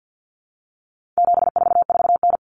Morse code, heard as a keyed tone, about 50 wpm, 700 Hz. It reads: 744N
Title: Common Cut Numbers - 4-Digit 50wpm